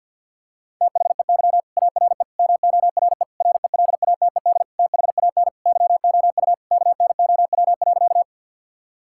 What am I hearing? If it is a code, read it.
Morse code, heard as a keyed tone, about 52 wpm, 700 Hz. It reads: THEY ARE MORE RELATED THAN YOU KNOW.
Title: Sentences from Top 1000 Words - Review Part 5 - 50wpm